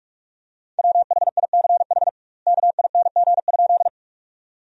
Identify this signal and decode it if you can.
Morse code, heard as a keyed tone, about 46 wpm, 700 Hz. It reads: WHICH KING?
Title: 2x Word-Spacing - Sentences from Top 400 Words - 45wpm